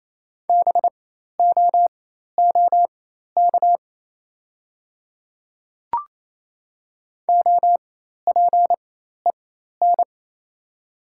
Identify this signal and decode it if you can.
Morse code, heard as a keyed tone, about 28 wpm, spaced as for 13 wpm, 690 Hz. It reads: BOOK OPEN
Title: 4x Character Spacing - Top 300 Words - 28wpm